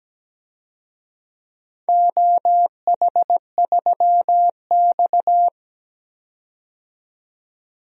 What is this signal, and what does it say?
Morse code, heard as a keyed tone, about 17 wpm, 700 Hz. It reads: OH3X
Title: DXCC-100 Call Signs - 4 Characters - 17wpm